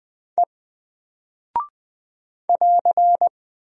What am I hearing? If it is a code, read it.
Morse code, heard as a keyed tone, about 20 wpm, 700 Hz. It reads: E <AR>
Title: Single Character (Letter:Number:Punctuation:Prosign) Alt 20wpm